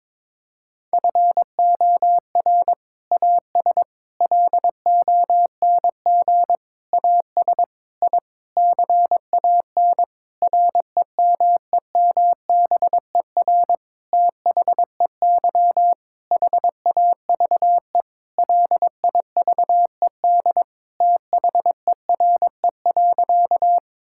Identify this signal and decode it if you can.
Morse code, heard as a keyed tone, about 22 wpm, 700 Hz. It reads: FOR AS LONG AS I CAN REMEMBER THEY HAVE LIVED THERE.
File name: Sentences with 7-Word Phrases - 22wpm